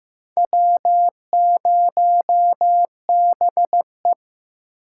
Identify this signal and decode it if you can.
Morse code, heard as a keyed tone, about 15 wpm, 700 Hz. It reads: W0BE
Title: USA Call Signs - 4 Characters - 2x Character Spacing - 15wpm